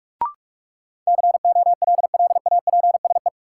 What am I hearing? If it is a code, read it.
Morse code, heard as a keyed tone, about 44 wpm, 700 Hz. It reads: COLLAPSE
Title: Sets of 5 Words Encore - New Word Drill - 45wpm